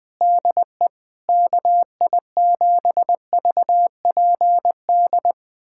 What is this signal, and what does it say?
Morse code, heard as a keyed tone, about 20 wpm, 700 Hz. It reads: DE KI7VPD
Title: USA Callsign DE Callsign 20wpm